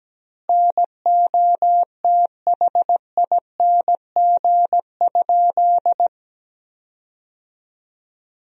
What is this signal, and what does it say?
Morse code, heard as a keyed tone, about 17 wpm, 700 Hz. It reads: NOTHING?
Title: 2x Word-Spacing - Sentences from Top 500 Words - 17wpm